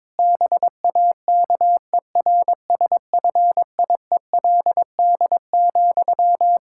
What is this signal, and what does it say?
Morse code, heard as a keyed tone, about 22 wpm, 700 Hz. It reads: BAKERSFIELD,
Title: Practice QSO - Exchange 1 - Beginning - 22wpm